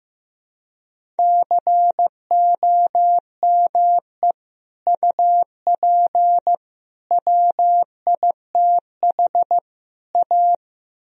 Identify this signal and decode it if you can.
Morse code, heard as a keyed tone, about 15 wpm, 700 Hz. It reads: COME UP WITH A
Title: Speed-Racing - Sets of 4 Words 15wpm